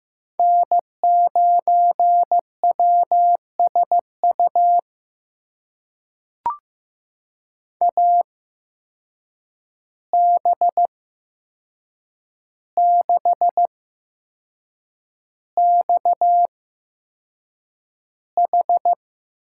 Morse code, heard as a keyed tone, about 15 wpm, spaced as for 6 wpm, 700 Hz. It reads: N9WSU AB6XH
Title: USA Call Signs - 5 Characters - 8x Character Spacing - 15wpm